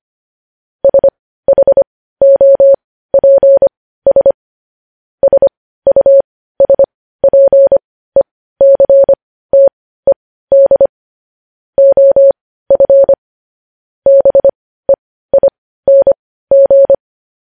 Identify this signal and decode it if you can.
Morse code, heard as a keyed tone, about 25 wpm, spaced as for 15 wpm, 550 Hz. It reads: SHOPS SUSPECTED OF BEING
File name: podcast / News headlines-1x-25x15